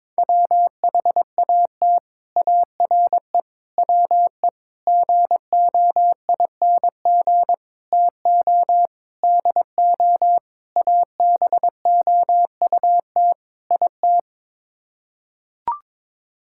Morse code, heard as a keyed tone, about 22 wpm, 700 Hz. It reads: WHAT ARE WE GOING TO DO ABOUT IT
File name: practice / 22wpm / Sets of 8 Words - 22wpm